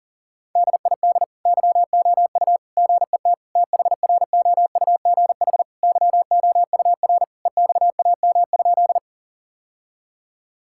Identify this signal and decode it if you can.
Morse code, heard as a keyed tone, about 40 wpm, 700 Hz. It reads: DID YOU GET THROUGH YOUR EXAM?
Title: Sentences from Top 1000 Words - Review Part 5 - 40wpm